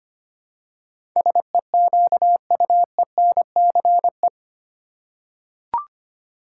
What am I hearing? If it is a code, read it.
Morse code, heard as a keyed tone, about 25 wpm, 690 Hz. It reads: SEQUENCE
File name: 8 Letter Words - First 3 Letters Given - 25wpm